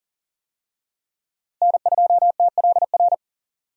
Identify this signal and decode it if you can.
Morse code, heard as a keyed tone, about 38 wpm, 700 Hz. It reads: N2TLR
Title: USA Call Signs - 5 Characters - Repeated 2x - 40wpm